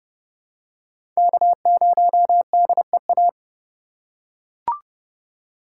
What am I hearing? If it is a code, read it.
Morse code, heard as a keyed tone, about 30 wpm, 700 Hz. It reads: K0DEA
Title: USA Call Signs - 5 Characters - 2x Character Spacing - 30wpm